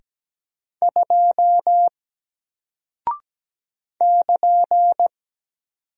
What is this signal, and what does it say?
Morse code, heard as a keyed tone, about 17 wpm, 700 Hz. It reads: 2 <KN>
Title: Single Character (Letter:Number:Punctuation:Prosign) Alt 17wpm